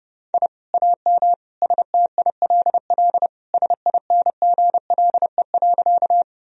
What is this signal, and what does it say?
Morse code, heard as a keyed tone, about 30 wpm, 700 Hz. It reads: I AM STILL SINGLE.
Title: Sentences from Top 1000 Words - Review Part 4 - 30wpm